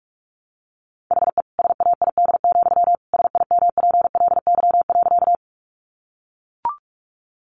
Morse code, heard as a keyed tone, about 44 wpm, 700 Hz. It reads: HE SAID, SIMPLY.
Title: Sentences from Top 600 Words 45wpm